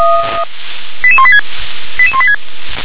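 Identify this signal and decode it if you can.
Five-tone selective call (ZVEI), heard as quick runs of five tones, tones of about 70 ms. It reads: 88186 88186
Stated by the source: ZVEI five-tone selective call received on SDR.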